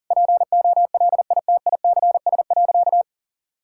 Morse code, heard as a keyed tone, about 40 wpm, 700 Hz. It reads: POLITICS.
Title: Sentences from Top 1000 Words - Review Part 6 - 40wpm